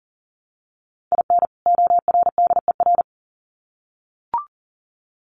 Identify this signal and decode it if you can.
Morse code, heard as a keyed tone, about 40 wpm, 700 Hz. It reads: IN ORDER
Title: Sets of 2 Words 40wpm